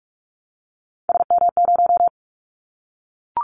Morse code, heard as a keyed tone, about 46 wpm, 700 Hz. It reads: SM0
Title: Callsign Prefixes for Top 25 DXCC - 45wpm